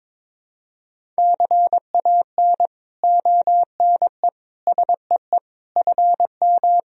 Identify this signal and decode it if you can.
Morse code, heard as a keyed tone, about 22 wpm, 700 Hz. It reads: CAN ONE SEE FM
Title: Sentences from Top 100 Words and Abbreviations 22wpm